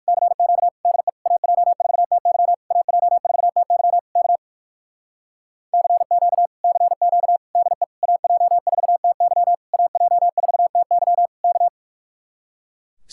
Morse code, heard as a keyed tone, about 60 wpm, 700 Hz. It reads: CQ DE AJ4TY AJ4TY K CQ CQ DE AJ4TY AJ4TY K